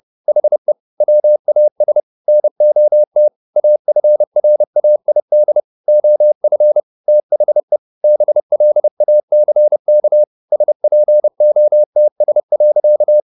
Morse code, heard as a keyed tone, about 30 wpm, 600 Hz. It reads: HE WAS NOT AFRAID OF THE BLACK SPOTS.